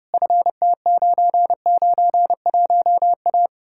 Morse code, heard as a keyed tone, about 37 wpm, 700 Hz. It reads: FT991A
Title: Speed-Racing Common QSO Phrases - 30wpm